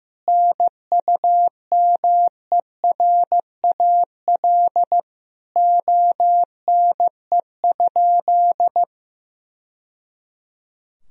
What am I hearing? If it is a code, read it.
Morse code, heard as a keyed tone, about 16 wpm, 700 Hz. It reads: NUMERAL ONE?
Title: Sentences from Top 1000 Words - Review Part 2 - 15wpm